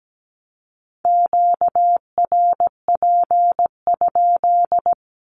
Morse code, heard as a keyed tone, about 17 wpm, 700 Hz. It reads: QRP?
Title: Q-Codes 17wpm